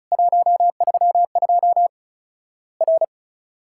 Morse code, heard as a keyed tone, about 34 wpm, 700 Hz. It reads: 132 R
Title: Worked All Europe DX Contest - QTC Exchange - 35wpm